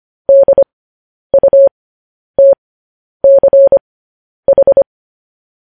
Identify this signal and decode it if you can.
Morse code, heard as a keyed tone, about 25 wpm, spaced as for 10 wpm, 550 Hz. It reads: DUTCH
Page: news headlines in morse code at 10 wpm